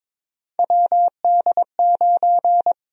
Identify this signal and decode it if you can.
Morse code, heard as a keyed tone, about 22 wpm, 700 Hz. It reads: WD9
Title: USA Call Sign Prefixes - 22wpm